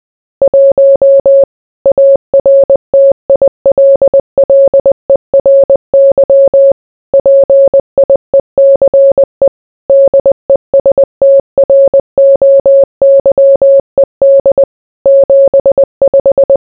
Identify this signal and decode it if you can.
Morse code, heard as a keyed tone, about 20 wpm, 550 Hz. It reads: 1 ARTILLERY PIECE DESTROYED 75